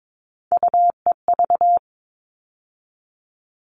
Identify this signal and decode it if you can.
Morse code, heard as a keyed tone, about 22 wpm, 700 Hz. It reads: UEV
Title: All Cut Numbers - 3-Digit 22wpm